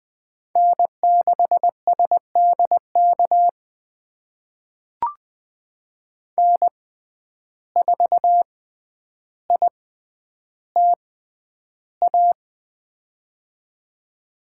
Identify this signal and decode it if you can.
Morse code, heard as a keyed tone, about 20 wpm, spaced as for 10 wpm, 700 Hz. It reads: N6SDK N4ITA
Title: USA Call Signs - 5 Characters - 6x Character Spacing - 20wpm